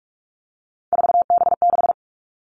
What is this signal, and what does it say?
Morse code, heard as a keyed tone, about 46 wpm, 700 Hz. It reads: 4B6
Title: Three Challenging Letter-Number Combinations 45wpm